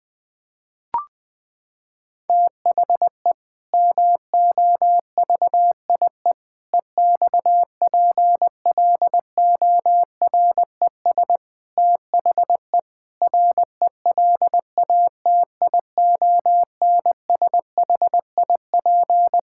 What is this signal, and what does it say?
Morse code, heard as a keyed tone, about 20 wpm, 700 Hz. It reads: THE MOVIE EXPLORES THE RELATIONSHIP
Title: Sentences with 6-Word Phrases - 20wpm